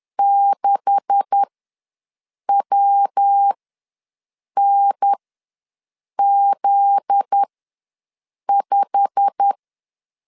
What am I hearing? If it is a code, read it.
Morse code, heard as a keyed tone, about 10 wpm, spaced as for 6 wpm, 800 Hz. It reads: 6WNZ5
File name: LEZIONI / 10 SIGNS / 10D COMA ws 10wpm 4314